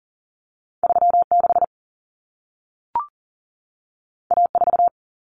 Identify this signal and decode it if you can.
Morse code, heard as a keyed tone, about 40 wpm, 700 Hz. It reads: 36 A4
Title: Common Cut Numbers - 2-Digit 40wpm